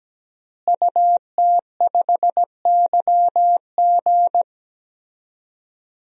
Morse code, heard as a keyed tone, about 17 wpm, 700 Hz. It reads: UT5YG
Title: DXCC-100 Call Signs - 5 Characters - Repeated 3x - 17wpm